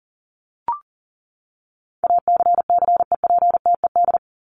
Morse code, heard as a keyed tone, about 40 wpm, 700 Hz. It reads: ACCEPTED
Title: ED Suffix - Top 100 Words with Base Words 40wpm